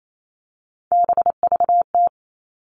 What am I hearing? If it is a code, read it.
Morse code, heard as a keyed tone, about 28 wpm, 700 Hz. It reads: BVT
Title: All Cut Numbers - 3-Digit 28wpm